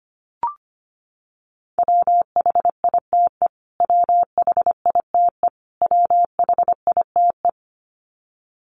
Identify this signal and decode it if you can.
Morse code, heard as a keyed tone, about 25 wpm, 700 Hz. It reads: WHITE WHITE WHITE
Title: Top 300 Words Repeated 25wpm